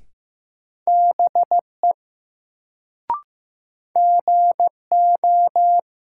Morse code, heard as a keyed tone, about 15 wpm, 700 Hz. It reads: BE GO
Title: Sets of 2 Words - New Word Drill - 15wpm